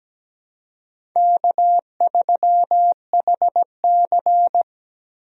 Morse code, heard as a keyed tone, about 17 wpm, 700 Hz. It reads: K3HC
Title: USA Call Signs - 4 Characters - Repeated 3x - 17wpm